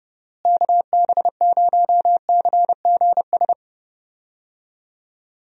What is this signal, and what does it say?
Morse code, heard as a keyed tone, about 32 wpm, 700 Hz. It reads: KB0CGS
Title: USA Call Signs - 6 Characters - Repeated 2x - 30wpm